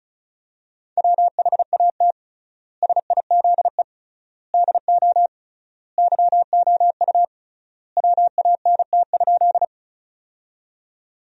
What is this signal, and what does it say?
Morse code, heard as a keyed tone, about 36 wpm, 700 Hz. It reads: WHAT SIZE DO YOU WANT?
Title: 3x Word-Spacing - Sentences from Top 400 Words - 35wpm